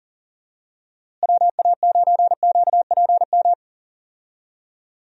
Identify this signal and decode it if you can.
Morse code, heard as a keyed tone, about 40 wpm, 700 Hz. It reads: WA9QPM
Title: USA Call Signs - 6 Characters - Repeated 3x - 40wpm